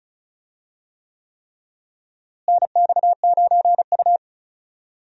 Morse code, heard as a keyed tone, about 34 wpm, 700 Hz. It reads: NX9U
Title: USA Call Signs - 4 Characters - Repeated 3x - 35wpm